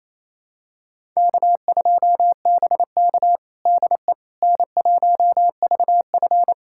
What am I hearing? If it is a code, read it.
Morse code, heard as a keyed tone, about 28 wpm, 700 Hz. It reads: K2BK DE N1VF
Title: USA Callsign DE Callsign - 4 Characters - 28wpm